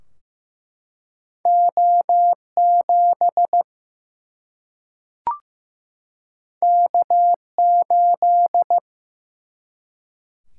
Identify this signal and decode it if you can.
Morse code, heard as a keyed tone, about 15 wpm, 700 Hz. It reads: O7 K8
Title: Single-Letter Single-Number 15wpm